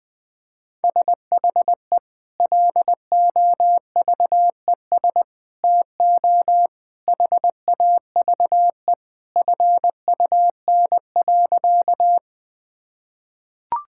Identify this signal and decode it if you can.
Morse code, heard as a keyed tone, about 20 wpm, 700 Hz. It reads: SHE LOVES TO HAVE FUN.
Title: Sentences with 2-Word Phrases - 20wpm